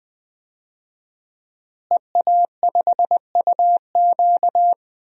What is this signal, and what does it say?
Morse code, heard as a keyed tone, about 20 wpm, 700 Hz. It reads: EA5UQ
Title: DXCC-100 Call Signs - 5 Characters - Repeated 2x - 20wpm